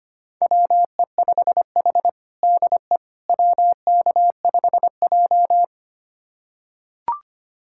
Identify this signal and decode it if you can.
Morse code, heard as a keyed tone, about 25 wpm, 700 Hz. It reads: WE5H DE WK5J
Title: USA Callsign DE Callsign - 4 Characters - 25wpm